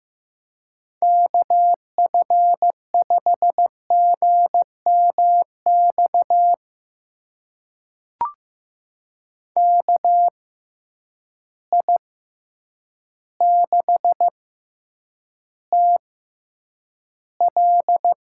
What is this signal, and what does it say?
Morse code, heard as a keyed tone, about 15 wpm, spaced as for 8 wpm, 700 Hz. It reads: KF5GMX KI6TL
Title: USA Call Signs - 6x Character Spacing - 15wpm